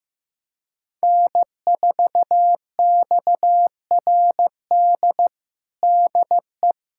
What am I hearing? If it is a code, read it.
Morse code, heard as a keyed tone, about 15 wpm, 700 Hz. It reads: N4XRD DE
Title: USA Callsign DE Callsign 15wpm